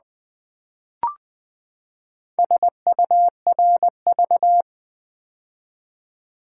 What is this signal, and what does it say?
Morse code, heard as a keyed tone, about 20 wpm, 700 Hz. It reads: SURV
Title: Build A Word - 4 Letters At A Time - 12-Letter Words - 20wpm